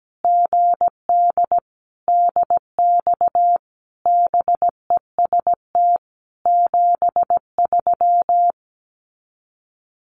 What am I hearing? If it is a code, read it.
Morse code, heard as a keyed tone, about 17 wpm, 700 Hz. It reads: GD DX BEST 73
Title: Common QSO Phrases 17wpm